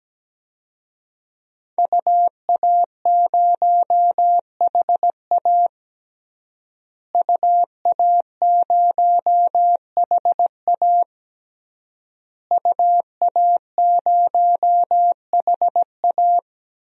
Morse code, heard as a keyed tone, about 17 wpm, 700 Hz. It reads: UA0HA UA0HA UA0HA
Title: DXCC-100 Call Signs - 5 Characters - Repeated 3x - 17wpm